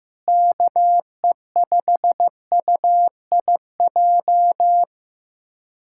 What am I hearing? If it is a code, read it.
Morse code, heard as a keyed tone, about 15 wpm, 700 Hz. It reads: KE5UIJ
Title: USA Callsign DE Callsign 15wpm